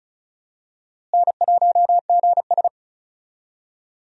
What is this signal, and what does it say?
Morse code, heard as a keyed tone, about 36 wpm, 700 Hz. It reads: N1GS
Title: USA Call Signs - 4 Characters - Repeated 2x - 35wpm